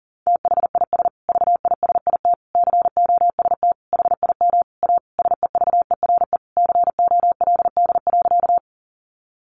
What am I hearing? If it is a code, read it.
Morse code, heard as a keyed tone, about 40 wpm, 700 Hz. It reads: THIS VISIT COST HIM A SEVERE COLD.
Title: Sentences from Top 1400 Words 40wpm